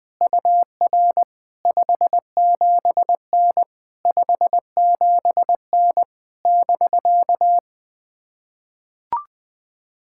Morse code, heard as a keyed tone, about 20 wpm, 700 Hz. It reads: UR 57N 57N <BK>
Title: POTA QSO Practice - 20wpm